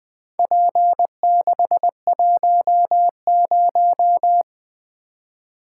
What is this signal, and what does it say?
Morse code, heard as a keyed tone, about 20 wpm, 700 Hz. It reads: P610
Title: Four Letter-Number Combinations - 20wpm